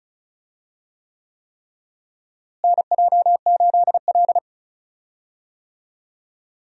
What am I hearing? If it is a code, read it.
Morse code, heard as a keyed tone, about 34 wpm, 700 Hz. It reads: NJ8L
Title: USA Call Signs - 4 Characters - Repeated 3x - 35wpm